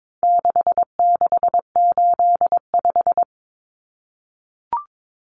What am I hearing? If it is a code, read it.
Morse code, heard as a keyed tone, about 22 wpm, 700 Hz. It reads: 6685
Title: Common Cut Numbers - 4-Digit 22wpm